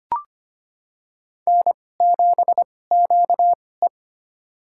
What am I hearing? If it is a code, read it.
Morse code, heard as a keyed tone, about 25 wpm, spaced as for 18 wpm, 700 Hz. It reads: N7QE
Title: USA Call Signs - 4 Characters - 2x Character Spacing - 25wpm